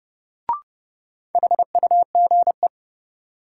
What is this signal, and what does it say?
Morse code, heard as a keyed tone, about 30 wpm, 700 Hz. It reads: HUGE
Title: Sentences from Top 1100 Words - New Word Drill - 30wpm